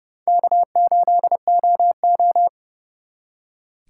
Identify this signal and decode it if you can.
Morse code, heard as a keyed tone, about 37 wpm, 700 Hz. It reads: K8OO
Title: Speed Racing - USA Call Signs 30wpm